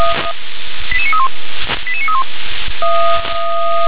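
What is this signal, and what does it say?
Five-tone selective call (ZVEI), heard as quick runs of five tones, tones of about 70 ms. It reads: 88931 88931
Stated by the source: ZVEI five-tone selective call 88771 received by SDR.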